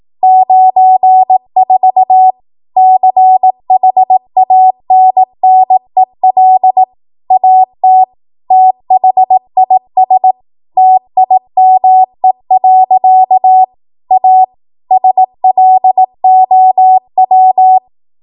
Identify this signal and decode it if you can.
Morse code, heard as a keyed tone, about 18 wpm, 750 Hz. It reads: 94 CHANNEL AT THIS TIME. A SLOW
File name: Morse / 18WPM